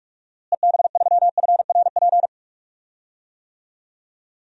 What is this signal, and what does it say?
Morse code, heard as a keyed tone, about 46 wpm, 700 Hz. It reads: EB3FRP
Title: DXCC-100 Call Signs - 6 Characters - Repeated 2x - 45wpm